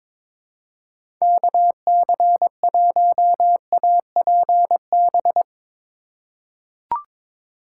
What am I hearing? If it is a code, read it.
Morse code, heard as a keyed tone, about 22 wpm, 700 Hz. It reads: KC1APB
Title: USA Call Signs - 6 Characters - 2x Character Spacing - 22wpm